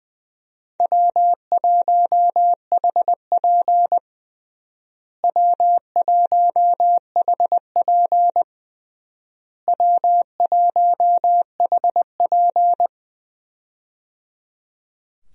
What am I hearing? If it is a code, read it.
Morse code, heard as a keyed tone, about 20 wpm, 700 Hz. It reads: W1HP W1HP W1HP
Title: USA Call Signs - 4 Characters - Repeated 3x - 20wpm